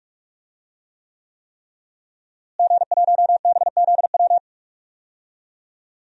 Morse code, heard as a keyed tone, about 41 wpm, 700 Hz. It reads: G1BZW
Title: DXCC-100 Call Signs - 5 Characters - Repeated 3x - 45wpm